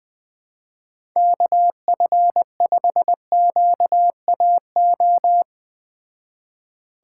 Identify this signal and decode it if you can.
Morse code, heard as a keyed tone, about 20 wpm, 700 Hz. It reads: KF5QAO
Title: Speed Racing - USA 2x3 Call Signs 17wpm